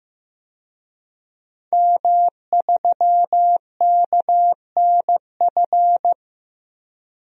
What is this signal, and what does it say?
Morse code, heard as a keyed tone, about 16 wpm, 700 Hz. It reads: M3KNF
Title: DXCC-100 Call Signs - 5 Characters - Repeated 2x - 15wpm